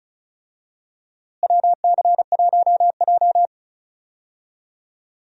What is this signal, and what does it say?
Morse code, heard as a keyed tone, about 36 wpm, 700 Hz. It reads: WC1J
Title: USA Call Signs - 4 Characters - Repeated 3x - 35wpm